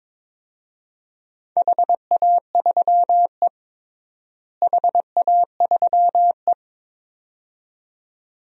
Morse code, heard as a keyed tone, about 22 wpm, 700 Hz. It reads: HA3E HA3E
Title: DXCC-100 Call Signs - 4 Characters - Repeated 2x - 22wpm